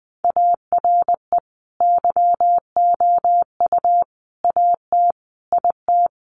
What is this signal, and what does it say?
Morse code, heard as a keyed tone, about 20 wpm, 700 Hz. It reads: ARE YOU AT IT
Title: Sentences from Top 200 Words 20wpm